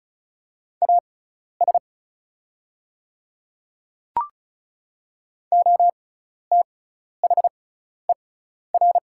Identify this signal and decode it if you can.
Morse code, heard as a keyed tone, about 36 wpm, spaced as for 12 wpm, 700 Hz. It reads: AS OTHER
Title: 6x Character Spacing - Top 100 Words - 35wpm